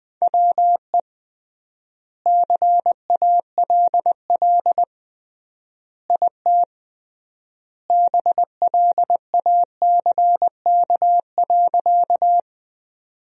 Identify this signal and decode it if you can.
Morse code, heard as a keyed tone, about 20 wpm, 700 Hz. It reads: WE CALL IT BLACK.
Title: 3x Word-Spacing - Sentences from Top 400 Words - 20wpm